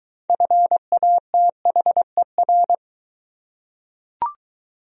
Morse code, heard as a keyed tone, about 20 wpm, 700 Hz. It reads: FATHER